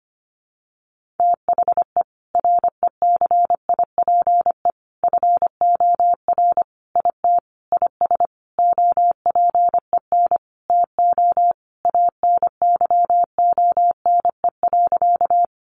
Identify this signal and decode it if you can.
Morse code, heard as a keyed tone, about 25 wpm, 700 Hz. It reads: THE RECIPE FOR IT IS OPEN TO ANYONE.
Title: Sentences from Top 1200 Words 25wpm